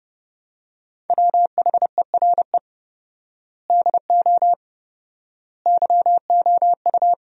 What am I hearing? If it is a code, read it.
Morse code, heard as a keyed tone, about 30 wpm, 700 Hz. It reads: WHERE DO YOU
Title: 4x Word-Spacing - Sentences from Top 200 Words - 30wpm